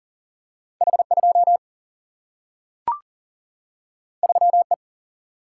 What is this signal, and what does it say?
Morse code, heard as a keyed tone, about 40 wpm, 700 Hz. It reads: H 2 3 E
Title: Single-Letter Single-Number 40wpm